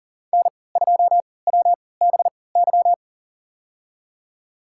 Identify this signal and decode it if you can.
Morse code, heard as a keyed tone, about 40 wpm, spaced as for 23 wpm, 700 Hz. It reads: N2WBY
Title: USA Call Signs - 5 Characters - 3x Character Spacing - 40wpm